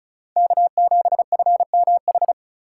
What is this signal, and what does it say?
Morse code, heard as a keyed tone, about 34 wpm, 700 Hz. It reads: K7FMH
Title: USA Call Signs - 5 Characters - 4x Character Spacing - 35wpm